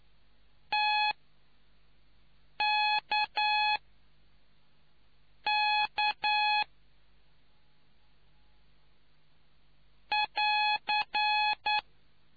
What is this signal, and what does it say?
Morse code, heard as a keyed tone, about 9 wpm, spaced as for 4 wpm, 840 Hz. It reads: TKK <AR>